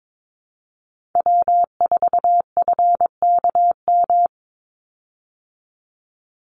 Morse code, heard as a keyed tone, about 22 wpm, 700 Hz. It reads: W4FKM
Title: Call Signs Repeated 3x - 22wpm